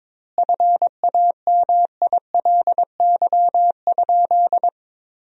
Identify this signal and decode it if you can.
Morse code, heard as a keyed tone, about 22 wpm, 700 Hz. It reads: FAMILY?
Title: 2x Word-Spacing - Sentences from Top 400 Words - 22wpm